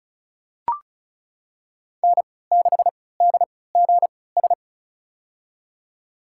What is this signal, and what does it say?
Morse code, heard as a keyed tone, about 34 wpm, spaced as for 20 wpm, 700 Hz. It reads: N6DGS
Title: USA Call Signs - 5 Characters - 3x Character Spacing - 35wpm